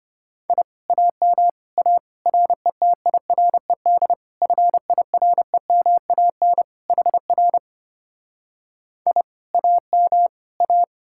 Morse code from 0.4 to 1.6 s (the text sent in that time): I AM